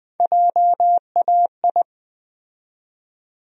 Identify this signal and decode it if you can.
Morse code, heard as a keyed tone, about 20 wpm, 700 Hz. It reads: JAI
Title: USA Call Sign Postfixes - 3 Letters 20wpm